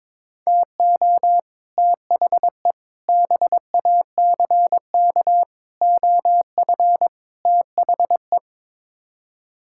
Morse code, heard as a keyed tone, about 22 wpm, 700 Hz. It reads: TO THE BACK OF THE
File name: practice / 22wpm / Build A Phrase - 5-Word Phrases - 22wpm